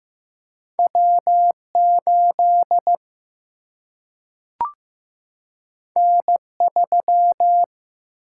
Morse code, heard as a keyed tone, about 15 wpm, 700 Hz. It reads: W8 N3
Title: Single-Letter Single-Number 15wpm